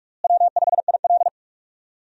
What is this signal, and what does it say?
Morse code, heard as a keyed tone, about 42 wpm, 700 Hz. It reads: W5IL
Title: USA Call Signs - 4 Characters - Repeated 2x - 45wpm